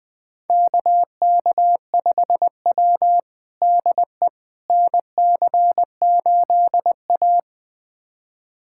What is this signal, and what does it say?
Morse code, heard as a keyed tone, about 20 wpm, 700 Hz. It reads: KK5W DE NC8A
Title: USA Callsign DE Callsign - 4 Characters - 20wpm